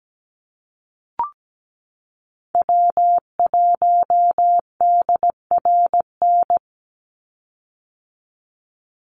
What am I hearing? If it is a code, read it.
Morse code, heard as a keyed tone, about 17 wpm, 700 Hz. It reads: W1DRN
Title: USA Call Signs 17wpm